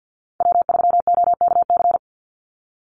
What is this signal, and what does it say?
Morse code, heard as a keyed tone, about 50 wpm, 700 Hz. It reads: W3OKC
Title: Call Signs Repeated 3x - 50wpm